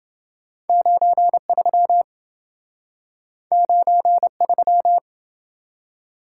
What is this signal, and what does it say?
Morse code, heard as a keyed tone, about 27 wpm, 700 Hz. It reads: 93 93